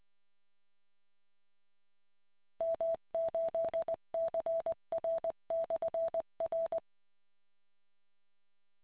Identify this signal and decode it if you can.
Morse code, heard as a keyed tone, about 20 wpm, 660 Hz. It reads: M8CR/R